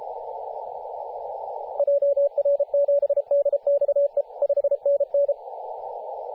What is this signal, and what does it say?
Morse code, heard as a keyed tone, about 31 wpm, 570 Hz, with radio noise. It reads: JR7DXE 5NN
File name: posts / 2017-jan-27-0632z-a35w-14-031mhz-cw